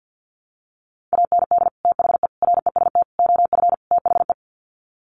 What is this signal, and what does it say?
Morse code, heard as a keyed tone, about 50 wpm, 700 Hz. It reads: AND THE REST OF THE
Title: Sets of 5 Words 50wpm